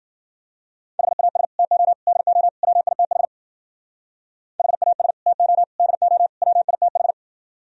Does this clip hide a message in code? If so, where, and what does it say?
Yes: Morse code from 1.0 to 7.1 s, heard as a keyed tone, about 52 wpm, 700 Hz. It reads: HAS TO DO WITH HAS TO DO WITH